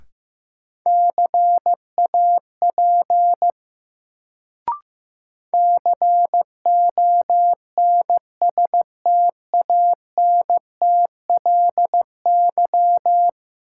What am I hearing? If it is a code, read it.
Morse code, heard as a keyed tone, about 15 wpm, 700 Hz. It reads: CAP CONSTANTLY
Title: Sentences from Top 1600 Words - New Word Drill Alt - 15wpm